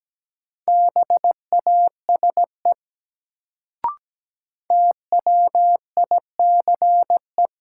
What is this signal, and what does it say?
Morse code, heard as a keyed tone, about 17 wpm, 700 Hz. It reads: BASE TWICE
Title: Sentences from Top 1600 Words - New Word Drill Alt - 17wpm